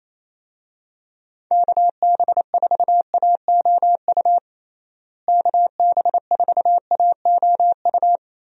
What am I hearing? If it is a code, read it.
Morse code, heard as a keyed tone, about 28 wpm, 700 Hz. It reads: KB4AOU KB4AOU
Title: USA Call Signs - 6 Characters - Repeated 3x - 28wpm